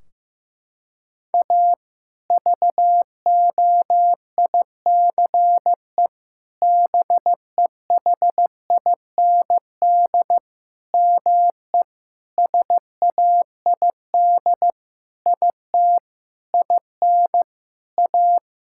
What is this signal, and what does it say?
Morse code, heard as a keyed tone, about 15 wpm, 700 Hz. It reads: A VOICE BEHIND ME SAID IT IN A
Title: Sentences from Top 1600 Words 15wpm